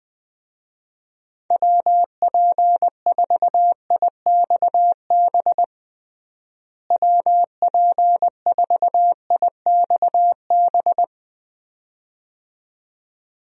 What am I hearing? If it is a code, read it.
Morse code, heard as a keyed tone, about 20 wpm, 700 Hz. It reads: WP4IXB WP4IXB
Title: USA Call Signs - 6 Characters - Repeated 2x - 20wpm